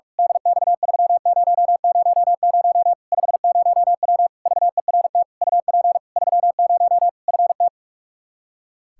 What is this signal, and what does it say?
Morse code, heard as a keyed tone, about 46 wpm, 700 Hz. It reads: DX3000 50W VERT UP 30 FT
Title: Practice QSO - Exchange 2 - Info - 45wpm